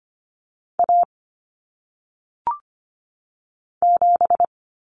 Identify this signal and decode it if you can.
Morse code, heard as a keyed tone, about 25 wpm, 700 Hz. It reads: A 7
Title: Common Cut Numbers - 1-Digit 25wpm